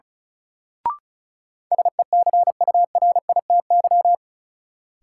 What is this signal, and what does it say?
Morse code, heard as a keyed tone, about 36 wpm, 700 Hz. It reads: SECURITY
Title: Sentences from Top 1000 Words - Review Part 4 - New Word Drill - 35wpm